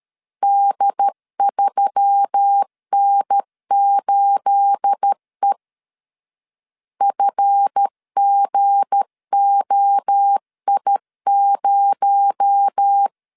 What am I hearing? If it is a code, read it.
Morse code, heard as a keyed tone, about 13 wpm, 800 Hz. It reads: D3N8E FGOI0
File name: LEZIONI / 9 28 12wpm 2351